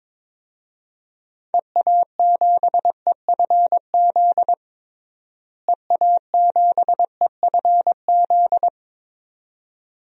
Morse code, heard as a keyed tone, about 22 wpm, 700 Hz. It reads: EA7EFZ EA7EFZ
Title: DXCC-100 Call Signs - 6 Characters - Repeated 2x - 22wpm